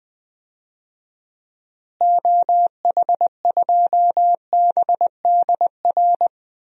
Morse code, heard as a keyed tone, about 20 wpm, 700 Hz. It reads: OH2BDR
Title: DXCC-100 Call Signs - 6 Characters - Repeated 2x - 20wpm